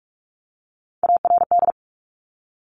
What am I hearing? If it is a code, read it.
Morse code, heard as a keyed tone, about 46 wpm, 700 Hz. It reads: ARD
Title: Top Three Letter Combinations in Words 45wpm